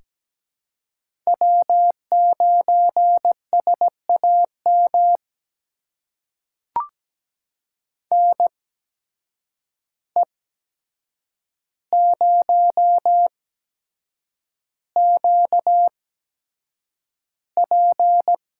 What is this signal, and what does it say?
Morse code, heard as a keyed tone, about 17 wpm, spaced as for 7 wpm, 700 Hz. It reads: W9SAM NE0QP
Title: USA Call Signs - 5 Characters - 8x Character Spacing - 17wpm